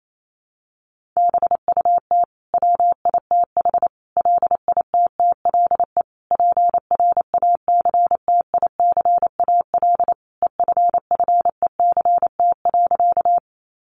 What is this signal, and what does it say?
Morse code, heard as a keyed tone, about 28 wpm, 700 Hz. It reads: BUT WITH LITTLE PRACTICAL EFFECT.
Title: Sentences from Top 1400 Words 28wpm